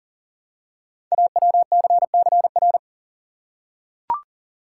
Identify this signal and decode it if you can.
Morse code, heard as a keyed tone, about 41 wpm, 690 Hz. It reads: AWCCR
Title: Code Groups - 5 Letters - 40wpm